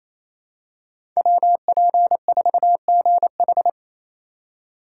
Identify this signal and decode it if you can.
Morse code, heard as a keyed tone, about 28 wpm, 690 Hz. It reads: WP4GH
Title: USA Call Signs - 5 Characters - 4x Character Spacing - 28wpm